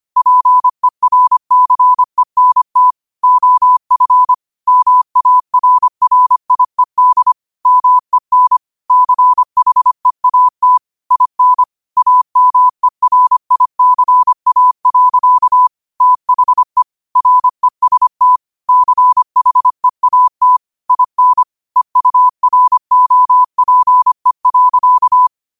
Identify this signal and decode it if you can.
Morse code, heard as a keyed tone, about 25 wpm, 1000 Hz. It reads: PERCENT OF MARRIED MEN CHEAT IN AMERICA. THE REST CHEAT IN EUROPE.